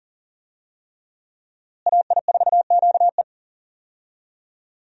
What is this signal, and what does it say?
Morse code, heard as a keyed tone, about 39 wpm, 700 Hz. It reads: AI4QE
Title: USA Call Signs - 5 Characters - Repeated 3x - 40wpm